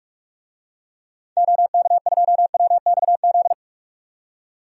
Morse code, heard as a keyed tone, about 41 wpm, 700 Hz. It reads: OK2WXZ
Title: DXCC-100 Call Signs - 6 Characters - 45wpm